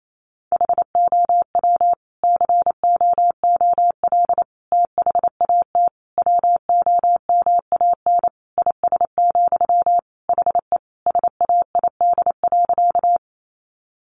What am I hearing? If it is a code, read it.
Morse code, heard as a keyed tone, about 28 wpm, 700 Hz. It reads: HOW COOL THAT WOMAN IS, HE SAID.
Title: Sentences from Top 1200 Words 28wpm